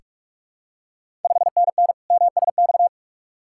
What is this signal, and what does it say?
Morse code, heard as a keyed tone, about 46 wpm, 700 Hz. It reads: 5NN MSX
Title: Ontario QSO Party Exchange 45wpm